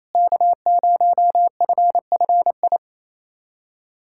Morse code, heard as a keyed tone, about 28 wpm, 700 Hz. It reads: K0FFI
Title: USA Callsign DE Callsign 28wpm